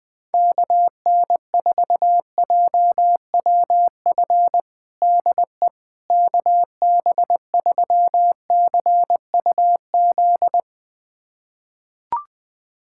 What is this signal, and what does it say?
Morse code, heard as a keyed tone, about 20 wpm, 700 Hz. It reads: KN4JWF DE KB3CUZ
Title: USA Callsign DE Callsign 20wpm